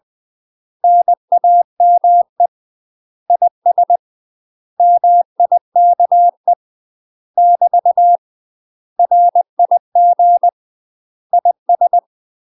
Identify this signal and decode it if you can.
Morse code, heard as a keyed tone, about 20 wpm, 700 Hz. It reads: NAME IS MIKE <BT> RIG IS
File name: UK MASTER QSOs TRAINING / 20WPM / UK MASTER QSOs 0015